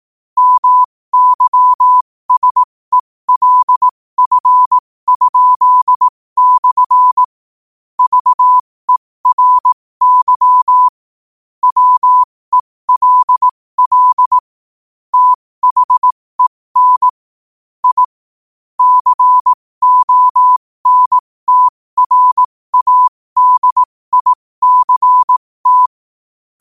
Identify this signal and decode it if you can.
Morse code, heard as a keyed tone, about 18 wpm, spaced as for 15 wpm, 1000 Hz. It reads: MYSELF?/ VERY WELL THEN I CONTRADICT
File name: qotdfiles / 15wpm